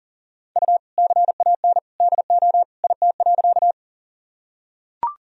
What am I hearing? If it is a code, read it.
Morse code, heard as a keyed tone, about 40 wpm, 700 Hz. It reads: U CAN DO IT.
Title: Sentences from Top 100 Words and Abbreviations 40wpm